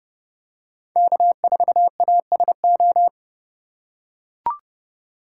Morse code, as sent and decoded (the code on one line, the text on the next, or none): -.- ....- .- ... ---
K4ASO